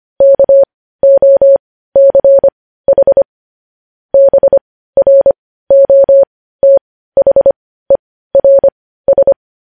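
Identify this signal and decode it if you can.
Morse code, heard as a keyed tone, about 25 wpm, spaced as for 15 wpm, 550 Hz. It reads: KOCH BROTHERS
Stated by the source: Morse code news headlines at 15.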